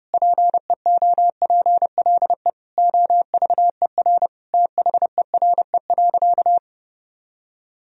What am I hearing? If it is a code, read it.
Morse code, heard as a keyed tone, about 30 wpm, 700 Hz. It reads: PEOPLE OVER THERE.
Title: Sentences from Top 1000 Words - Review Part 1 - 30wpm